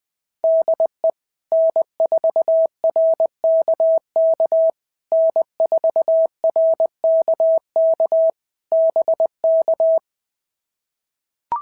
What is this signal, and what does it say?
Morse code, heard as a keyed tone, about 21 wpm, 660 Hz. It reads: DE N4RKK N4RKK BK
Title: POTA QSO Practice - 20wpm